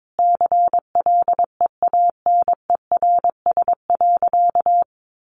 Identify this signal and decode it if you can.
Morse code, heard as a keyed tone, about 22 wpm, 700 Hz. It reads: CLEANERS.
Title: Additional Easy Phrases 22wpm